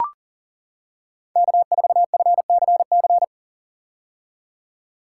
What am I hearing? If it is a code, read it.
Morse code, heard as a keyed tone, about 39 wpm, 700 Hz. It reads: K4FCC
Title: K1USN SST QSO Practice - 40wpm